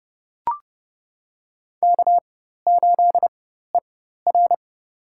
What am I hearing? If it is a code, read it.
Morse code, heard as a keyed tone, about 30 wpm, spaced as for 14 wpm, 700 Hz. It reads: K8ER
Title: USA Call Signs - 4 Characters - 4x Character Spacing - 30wpm